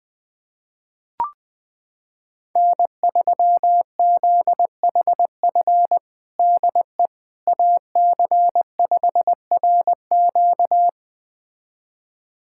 Morse code, heard as a keyed tone, about 20 wpm, 700 Hz. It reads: N3ZHF DE AC5RQ
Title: USA Callsign DE Callsign - 5 Characters - 20wpm